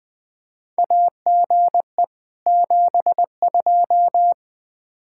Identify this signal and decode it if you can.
Morse code, heard as a keyed tone, about 20 wpm, 700 Hz. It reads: AGE 72
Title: Practice QSO - Exchange 3 - More Info - 20wpm